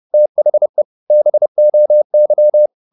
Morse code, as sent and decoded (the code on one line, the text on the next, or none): - .... / -... --- -.--
THE BOY